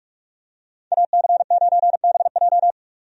Morse code, heard as a keyed tone, about 42 wpm, 700 Hz. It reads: AC9BJ
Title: USA Call Signs - 5 Characters - 45wpm